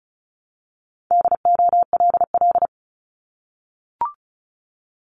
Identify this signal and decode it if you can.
Morse code, heard as a keyed tone, about 36 wpm, 700 Hz. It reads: DOLL
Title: Multi-letter Phonemes 35wpm